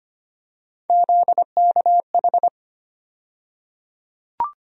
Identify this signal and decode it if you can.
Morse code, heard as a keyed tone, about 25 wpm, 700 Hz. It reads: ZKH
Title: USA Call Sign Postfixes - 3 Letters 25wpm